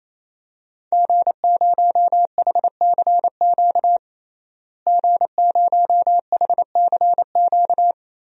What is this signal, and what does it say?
Morse code, heard as a keyed tone, about 28 wpm, 700 Hz. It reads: G0HCQ G0HCQ
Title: DXCC-100 Call Signs - 5 Characters - Repeated 3x - 28wpm